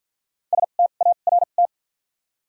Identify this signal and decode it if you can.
Morse code, heard as a keyed tone, about 53 wpm, 700 Hz. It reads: S T A R T
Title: 2x Character Spacing - Top 300 Words - 50wpm